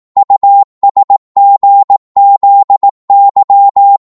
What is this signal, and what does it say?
Morse code, heard as a keyed tone, about 18 wpm, 800 Hz. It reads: USGZY
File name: Morse / ltr-18wpm